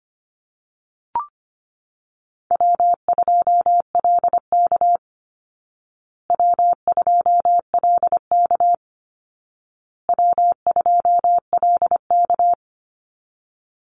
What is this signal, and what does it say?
Morse code, heard as a keyed tone, about 25 wpm, 700 Hz. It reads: W2LK W2LK W2LK
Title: Easy Call Signs Repeated 3x - 25wpm